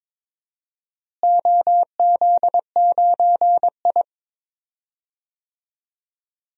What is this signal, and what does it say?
Morse code, heard as a keyed tone, about 21 wpm, 700 Hz. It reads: OZ9I
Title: DXCC-100 Call Signs - 4 Characters - Repeated 3x - 22wpm